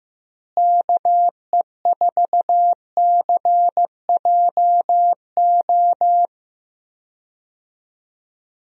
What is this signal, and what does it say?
Morse code, heard as a keyed tone, about 15 wpm, 700 Hz. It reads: KE4CJO
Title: USA Call Signs - 6 Characters - Repeated 2x - 15wpm